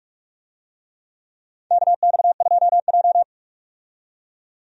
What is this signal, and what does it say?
Morse code, heard as a keyed tone, about 46 wpm, 700 Hz. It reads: KX2J
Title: USA Call Signs - 4 Characters - 45wpm